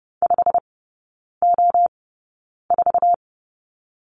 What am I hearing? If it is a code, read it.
Morse code, heard as a keyed tone, about 30 wpm, spaced as for 9 wpm, 700 Hz. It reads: 5O4
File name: course / 30:10wpm / 087 Lesson 15e - Number 5 - Three Characters 30wpm